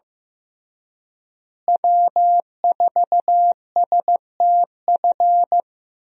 Morse code, heard as a keyed tone, about 15 wpm, 700 Hz. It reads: W4STF
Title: USA Call Signs - 5 Characters - Repeated 2x - 15wpm